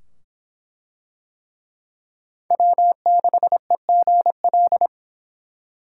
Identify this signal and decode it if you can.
Morse code, heard as a keyed tone, about 24 wpm, 700 Hz. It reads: W6EGL